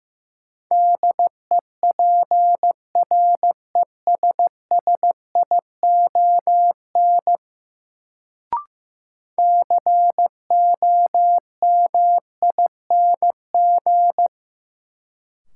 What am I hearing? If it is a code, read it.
Morse code, heard as a keyed tone, about 15 wpm, 700 Hz. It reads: DEPRESSION COMING
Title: Sentences from Top 1300 Words - New Word Drill Alt - 15wpm